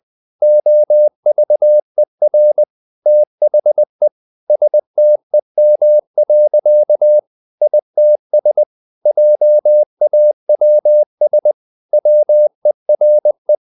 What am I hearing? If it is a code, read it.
Morse code, heard as a keyed tone, about 20 wpm, 600 Hz. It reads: OVER THE STEM. ITS JAWS WERE
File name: the old man and the sea part 2 20wpm